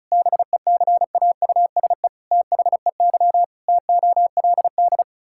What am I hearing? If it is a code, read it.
Morse code, heard as a keyed tone, about 34 wpm, 700 Hz. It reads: BECAUSE THEY TOLD
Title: Pangram Challenge v2 - 35wpm